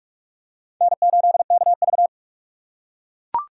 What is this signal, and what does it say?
Morse code, heard as a keyed tone, about 42 wpm, 700 Hz. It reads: N8KV
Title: USA Call Signs - 4 Characters - 2x Character Spacing - 45wpm